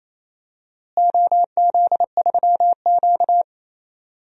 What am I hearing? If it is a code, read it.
Morse code, heard as a keyed tone, about 28 wpm, 700 Hz. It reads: OZ3Q
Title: DXCC-100 Call Signs - 4 Characters - Repeated 3x - 28wpm